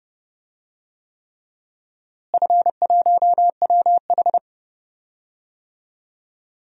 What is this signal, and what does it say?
Morse code, heard as a keyed tone, about 30 wpm, 700 Hz. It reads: F1WH
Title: DXCC-100 Call Signs - 4 Characters - Repeated 3x - 30wpm